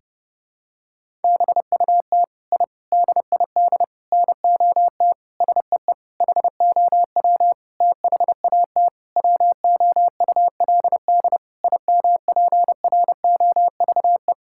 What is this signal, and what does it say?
Morse code, heard as a keyed tone, about 32 wpm, 700 Hz. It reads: BUT I DID NOT SEE HOW THAT WOULD IMPROVE
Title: Sentences from Top 1000 Words - Review Part 5 - 30wpm